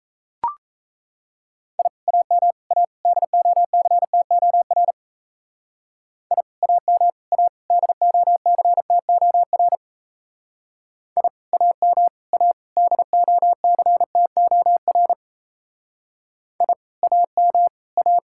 Morse code, heard as a keyed tone, about 34 wpm, 700 Hz. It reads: I AM A DOCTOR I AM A DOCTOR I AM A DOCTOR I AM A
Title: Speed-Racing Common QSO Phrases Encore - 28wpm